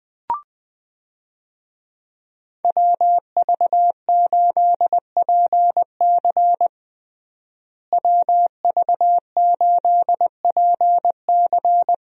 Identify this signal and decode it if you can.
Morse code, heard as a keyed tone, about 20 wpm, 700 Hz. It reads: WV8PC WV8PC
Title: USA Call Signs - 5 Characters - Repeated 3x - 20wpm